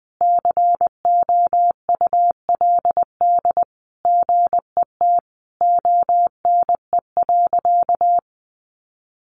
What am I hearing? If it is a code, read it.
Morse code, heard as a keyed tone, about 20 wpm, 700 Hz. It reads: COULD GET ONE.
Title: Sentences from Top 1400 Words 20wpm